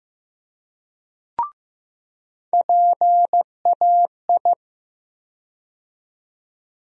Morse code, heard as a keyed tone, about 15 wpm, 700 Hz. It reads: PAI
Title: USA Call Sign Postfixes - 3 Letters 15wpm